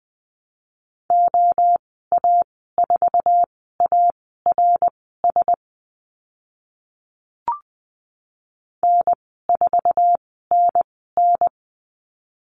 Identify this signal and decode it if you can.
Morse code, heard as a keyed tone, about 20 wpm, spaced as for 14 wpm, 700 Hz. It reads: OA4ARS N4NN
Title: 055 Lesson 10g - Letter r - Call Signs 20wpm